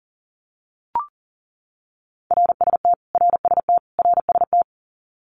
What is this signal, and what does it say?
Morse code, heard as a keyed tone, about 40 wpm, 700 Hz. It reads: RST RST RST
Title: Abbreviations Repeated 3x - 40wpm